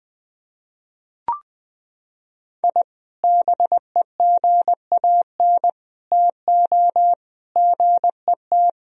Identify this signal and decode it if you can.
Morse code, heard as a keyed tone, about 20 wpm, 700 Hz. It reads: I BEGAN TO GET
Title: Sentences from Top 1000 Words - Review Part 4 - 20wpm